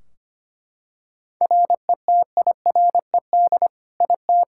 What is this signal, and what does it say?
Morse code, heard as a keyed tone, about 25 wpm, 700 Hz. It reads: RETIRED IT
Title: Practice QSO - Exchange 3 - More Info - 25wpm